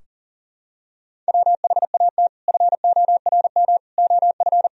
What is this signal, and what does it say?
Morse code, heard as a keyed tone, about 37 wpm, 700 Hz. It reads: WHAT FORM OF
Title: Sentences from Top 1000 Words - Review Part 3 - 40wpm